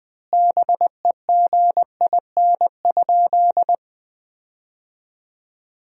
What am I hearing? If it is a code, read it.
Morse code, heard as a keyed tone, about 20 wpm, 700 Hz. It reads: BEGIN?
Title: 4x Word-Spacing - Sentences from Top 300 Words - 20wpm